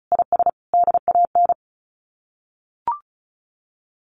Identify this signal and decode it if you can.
Morse code, heard as a keyed tone, about 34 wpm, 700 Hz. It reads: IS DAN
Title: Common QSO Phrases 35wpm